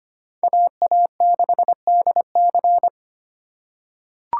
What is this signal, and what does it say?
Morse code, heard as a keyed tone, about 25 wpm, 700 Hz. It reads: AA6DC
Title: USA Call Signs - 5 Characters - 3x Character Spacing - 25wpm